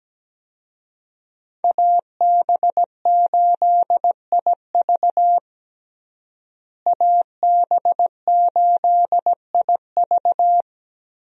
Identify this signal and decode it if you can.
Morse code, heard as a keyed tone, about 17 wpm, 700 Hz. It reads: AB8IV AB8IV
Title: USA Call Signs - 5 Characters - Repeated 3x - 17wpm